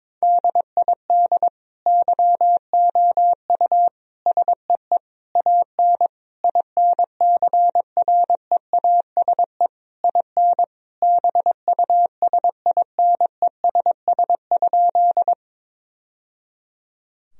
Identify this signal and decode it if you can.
Morse code, heard as a keyed tone, about 22 wpm, 700 Hz. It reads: DID YOU SEE AN INCREASE IN BUSINESS?
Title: Sentences from Top 1000 Words - Review Part 4 - 22wpm